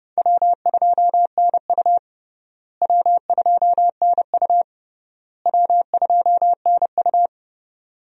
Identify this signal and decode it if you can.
Morse code, heard as a keyed tone, about 29 wpm, 700 Hz. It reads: W2NU W2NU W2NU
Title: USA Call Signs - 4 Characters - Repeated 3x - 30wpm